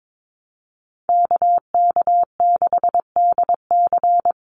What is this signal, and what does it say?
Morse code, heard as a keyed tone, about 22 wpm, 700 Hz. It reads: KK6DC
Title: USA Call Signs 22wpm